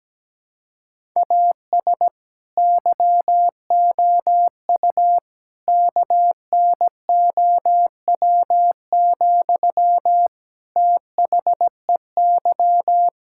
Morse code, heard as a keyed tone, about 17 wpm, 700 Hz. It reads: AS YOU KNOW, THEY
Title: Sentences from Top 1000 Words - Review Part 5 - 17wpm